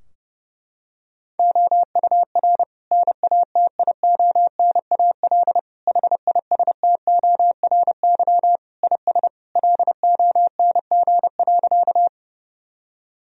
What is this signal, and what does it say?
Morse code, heard as a keyed tone, about 30 wpm, 700 Hz. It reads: OUR NATIONAL HISTORY IS LONG.
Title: Sentences from Top 1000 Words - Review Part 4 - 30wpm